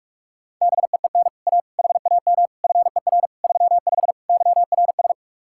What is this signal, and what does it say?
Morse code, heard as a keyed tone, about 45 wpm, 700 Hz. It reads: BEEN A HAM FER 35 YRS